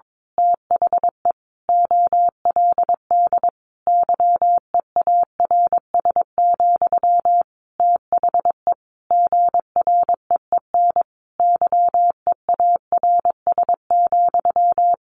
Morse code, heard as a keyed tone, about 22 wpm, 700 Hz. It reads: THE OLD YEARS, THE GREEN YEARS,
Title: A Sound Of Thunder 22wpm